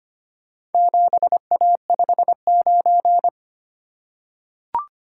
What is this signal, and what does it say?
Morse code, heard as a keyed tone, about 25 wpm, 700 Hz. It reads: 7A59
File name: practice / 25wpm / Four Letter-Number Combinations - 25wpm